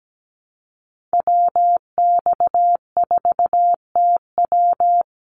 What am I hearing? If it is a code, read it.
Morse code, heard as a keyed tone, about 17 wpm, 700 Hz. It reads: WX4TW
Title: NA SOTA Activator Callsigns 17wpm